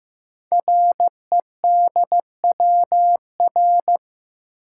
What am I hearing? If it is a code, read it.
Morse code, heard as a keyed tone, about 15 wpm, 700 Hz. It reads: REDWR
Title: Code Groups - 5 Letters - 15wpm